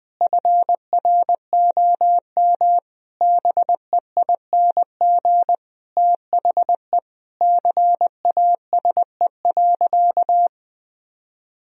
Morse code, heard as a keyed tone, about 20 wpm, 700 Hz. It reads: FROM BEING THE CASE.
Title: Sentences from Top 1000 Words - Review Part 3 - 20wpm